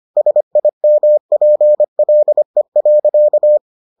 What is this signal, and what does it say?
Morse code, heard as a keyed tone, about 25 wpm, 600 Hz. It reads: SIMPLE.